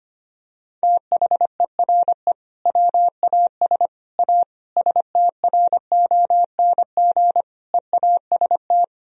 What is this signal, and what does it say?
Morse code, heard as a keyed tone, about 25 wpm, 700 Hz. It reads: THERE WAS A STRONG EAST
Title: Sentences from Top 1000 Words - Review Part 3 - 25wpm